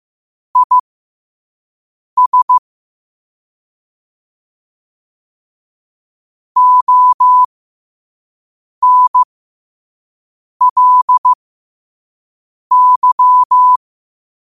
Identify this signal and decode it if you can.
Morse code, heard as a keyed tone, about 15 wpm, spaced as for 5 wpm, 1000 Hz. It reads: IS ONLY
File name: qotdfiles / 5wpm